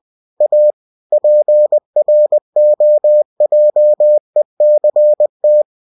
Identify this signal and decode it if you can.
Morse code, heard as a keyed tone, about 20 wpm, 600 Hz. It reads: A PROJECT